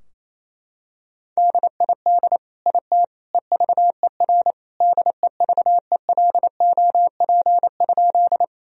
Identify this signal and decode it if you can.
Morse code, heard as a keyed tone, about 28 wpm, 700 Hz. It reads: DID IT EVER DEVELOP?
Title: Sentences from Top 1000 Words - Review Part 2 - 28wpm